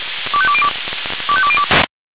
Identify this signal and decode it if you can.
Five-tone selective call (ZVEI), heard as quick runs of five tones, tones of about 70 ms. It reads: 25202 25202